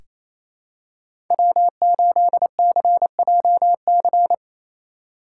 Morse code, heard as a keyed tone, about 28 wpm, 700 Hz. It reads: W8CJC
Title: USA Call Signs - 5 Characters - 2x Character Spacing - 28wpm